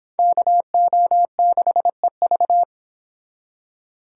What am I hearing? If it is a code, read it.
Morse code, heard as a keyed tone, about 31 wpm, 700 Hz. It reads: KO6EV